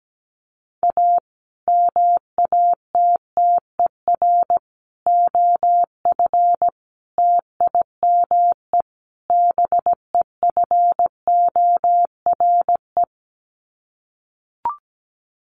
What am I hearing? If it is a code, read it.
Morse code, heard as a keyed tone, about 17 wpm, 700 Hz. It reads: A MATTER OF TIME BEFORE
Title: Sets of 5 Words Encore 17wpm